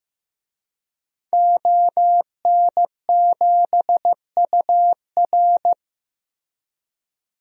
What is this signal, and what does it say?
Morse code, heard as a keyed tone, about 16 wpm, 700 Hz. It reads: ON7UR
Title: DXCC-100 Call Signs - 5 Characters - Repeated 3x - 15wpm